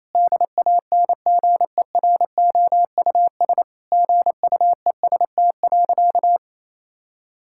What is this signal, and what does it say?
Morse code, heard as a keyed tone, about 27 wpm, 700 Hz. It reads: DANGEROUS GUEST.
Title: Sentences from Top 1600 Words 28wpm